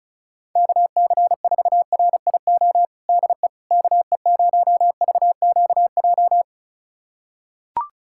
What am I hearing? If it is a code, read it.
Morse code, heard as a keyed tone, about 34 wpm, 700 Hz. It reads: KC4RIO DE KE0VQJ
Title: USA Callsign DE Callsign 35wpm